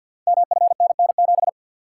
Morse code, heard as a keyed tone, about 51 wpm, 700 Hz. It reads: MFNN7
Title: Code Groups - 5 Letter:Numbers - 50wpm